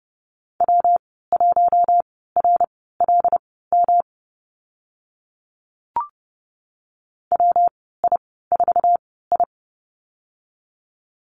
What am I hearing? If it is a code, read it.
Morse code, heard as a keyed tone, about 30 wpm, spaced as for 17 wpm, 700 Hz. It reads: W1RLM WI4I
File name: Lesson 19g - Letter w - Call Signs 30wpm